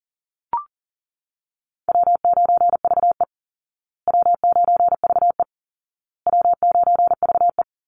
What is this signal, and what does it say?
Morse code, heard as a keyed tone, about 40 wpm, 700 Hz. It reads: W9VE W9VE W9VE
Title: Easy Call Signs Repeated 3x - 40wpm